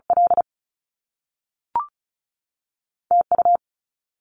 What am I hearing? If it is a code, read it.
Morse code, heard as a keyed tone, about 36 wpm, spaced as for 3 wpm, 700 Hz. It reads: LT U
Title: Top Two Letter Combinations in Words 35wpm